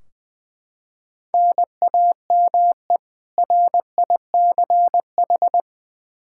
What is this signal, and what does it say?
Morse code, heard as a keyed tone, about 20 wpm, 700 Hz. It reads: NAME RICH
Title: K1USN SST QSO Practice 20wpm